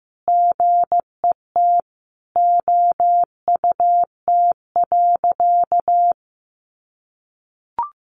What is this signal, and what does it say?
Morse code, heard as a keyed tone, about 14 wpm, 700 Hz. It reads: GET OUT.
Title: A Sound Of Thunder 15wpm